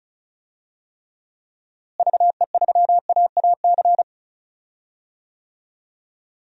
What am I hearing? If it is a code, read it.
Morse code, heard as a keyed tone, about 36 wpm, 700 Hz. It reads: VE3AAC
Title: DXCC-100 Call Signs - 6 Characters - Repeated 3x - 35wpm